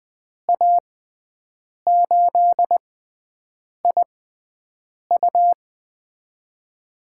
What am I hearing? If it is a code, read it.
Morse code, heard as a keyed tone, about 20 wpm, spaced as for 7 wpm, 700 Hz. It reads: A8IU
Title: USA Call Signs - 5 Characters - 6x Character Spacing - 20wpm